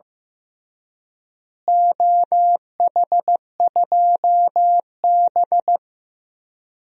Morse code, heard as a keyed tone, about 15 wpm, 700 Hz. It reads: OH2B
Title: DXCC-100 Call Signs - 4 Characters - Repeated 2x - 15wpm